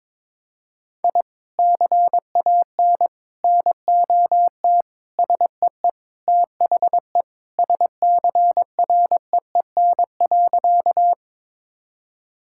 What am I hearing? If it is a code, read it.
Morse code, heard as a keyed tone, about 22 wpm, 700 Hz. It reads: I CAN NOT SEE THE SCREEN.
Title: Sentences from Top 1000 Words - Review Part 6 - 22wpm